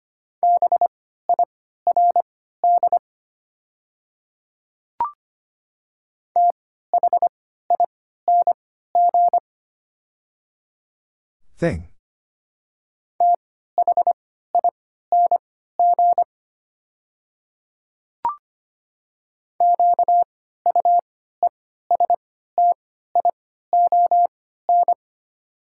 Morse code, heard as a keyed tone, about 25 wpm, spaced as for 14 wpm, 700 Hz. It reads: BIRD THING THING QUESTION
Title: 3x Character Spacing - Top 500 Words - 25wpm